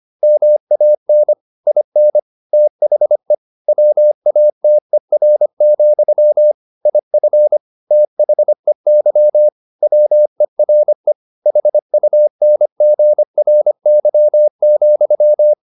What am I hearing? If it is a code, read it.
Morse code, heard as a keyed tone, about 25 wpm, 600 Hz. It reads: MAN IN THE WATER, IF THEY WERE HUNGRY,